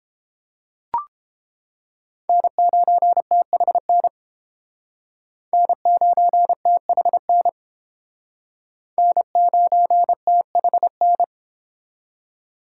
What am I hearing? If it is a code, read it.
Morse code, heard as a keyed tone, about 29 wpm, 700 Hz. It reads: N9THN N9THN N9THN